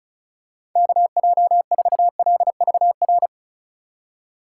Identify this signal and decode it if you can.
Morse code, heard as a keyed tone, about 34 wpm, 700 Hz. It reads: KJ4LVR
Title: USA Call Signs - 6 Characters - 2x Character Spacing - 35wpm